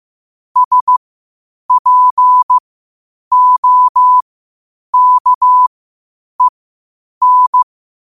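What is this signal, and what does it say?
Morse code, heard as a keyed tone, about 15 wpm, spaced as for 8 wpm, 1000 Hz. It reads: SPOKEN
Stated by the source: Morse code quotes at 8 WPM.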